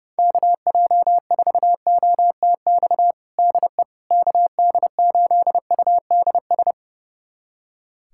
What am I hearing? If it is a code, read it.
Morse code, heard as a keyed tone, about 30 wpm, 700 Hz. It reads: KJ4OTX DE KD8UDS
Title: USA Callsign DE Callsign - 6 Characters - 30wpm